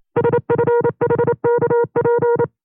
Morse code, heard as a keyed tone, about 28 wpm, 480 Hz. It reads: SFHKP
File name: FreeFiles / CW-practice / 25WPM-side1